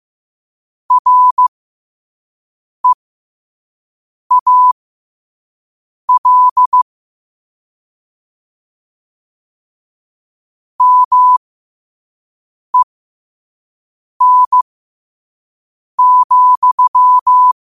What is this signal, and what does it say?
Morse code, heard as a keyed tone, about 15 wpm, spaced as for 5 wpm, 1000 Hz. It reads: REAL MEN,
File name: qotdfiles / 5wpm